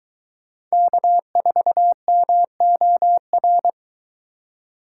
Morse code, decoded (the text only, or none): K4MOR